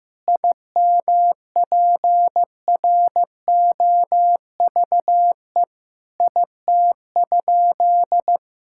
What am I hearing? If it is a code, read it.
Morse code, heard as a keyed tone, about 15 wpm, 700 Hz. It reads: IMPROVE IT?
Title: Sentences from Top 1000 Words - Review Part 5 - 15wpm